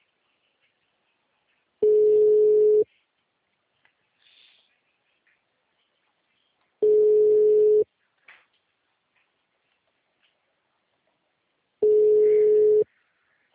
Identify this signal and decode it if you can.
Ringback tone (Europe, 425 Hz).